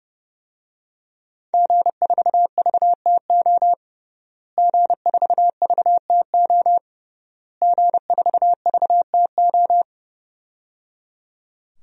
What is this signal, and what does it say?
Morse code, heard as a keyed tone, about 30 wpm, 700 Hz. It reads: G4VTO G4VTO G4VTO
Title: DXCC-100 Call Signs - 5 Characters - Repeated 3x - 30wpm